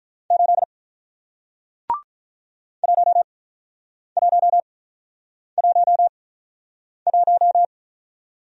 Morse code, heard as a keyed tone, about 46 wpm, spaced as for 9 wpm, 700 Hz. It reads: 8 1111